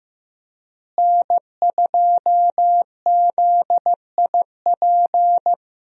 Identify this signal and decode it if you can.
Morse code, heard as a keyed tone, about 15 wpm, 700 Hz. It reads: N2ZIP
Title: USA Call Signs - 5 Characters - Repeated 3x - 15wpm